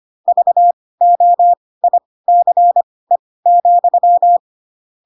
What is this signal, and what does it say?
Morse code, heard as a keyed tone, about 25 wpm, spaced as for 18 wpm, 700 Hz. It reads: VOICE,